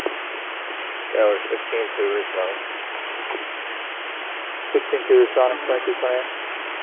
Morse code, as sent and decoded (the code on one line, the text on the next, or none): none